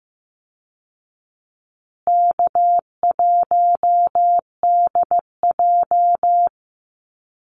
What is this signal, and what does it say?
Morse code, heard as a keyed tone, about 15 wpm, 700 Hz. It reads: K1DJ
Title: Easy Call Signs Repeated 2x - 15wpm